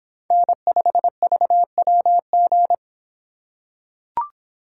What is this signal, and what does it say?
Morse code, heard as a keyed tone, about 25 wpm, 700 Hz. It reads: N5VWG